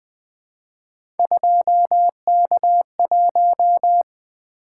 Morse code, heard as a keyed tone, about 20 wpm, 700 Hz. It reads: 2K1
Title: Three Letter-Number Combinations - 20wpm